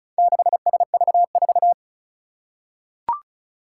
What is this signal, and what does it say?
Morse code, heard as a keyed tone, about 36 wpm, 700 Hz. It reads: 6SV4
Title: Four Challenging Letter-Number Combinations 35wpm